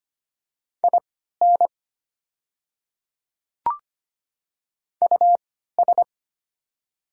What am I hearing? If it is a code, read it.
Morse code, heard as a keyed tone, about 25 wpm, spaced as for 14 wpm, 700 Hz. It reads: IN US
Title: 3x Character Spacing - Top 300 Words - 25wpm